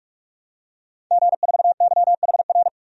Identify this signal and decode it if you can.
Morse code, heard as a keyed tone, about 45 wpm, 700 Hz. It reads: G4YHR
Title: DXCC-100 Call Signs - 5 Characters - 45wpm